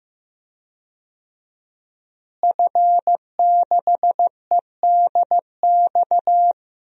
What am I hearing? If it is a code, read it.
Morse code, heard as a keyed tone, about 15 wpm, 700 Hz. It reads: F6EDX
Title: DXCC-100 Call Signs - 5 Characters - Repeated 3x - 15wpm